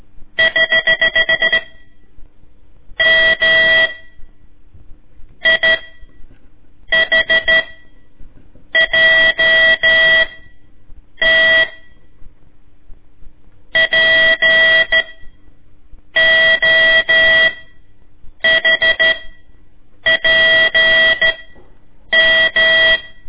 Morse code, heard as a keyed tone, about 13 wpm, spaced as for 7 wpm, 1950 Hz. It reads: <HH> M I H JT P OHPM
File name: ik1weg2 / CORSI CW / lez4-1